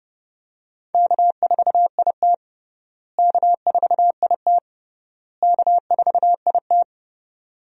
Morse code, heard as a keyed tone, about 32 wpm, 700 Hz. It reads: K4IT K4IT K4IT
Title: USA Call Signs - 4 Characters - Repeated 3x - 30wpm